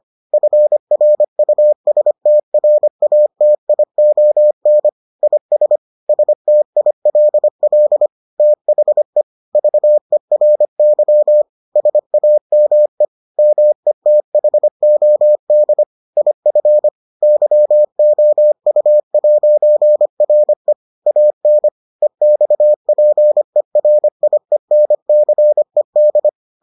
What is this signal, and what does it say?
Morse code, heard as a keyed tone, about 25 wpm, 600 Hz. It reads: FRUSTRATION IS STILL THE VERY SAME METHOD IF YOU'RE AN EXPERIENCED